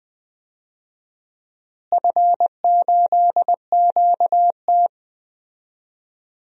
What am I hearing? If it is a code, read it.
Morse code, heard as a keyed tone, about 20 wpm, 700 Hz. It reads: F8QT
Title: DXCC-100 Call Signs - 4 Characters - Repeated 3x - 20wpm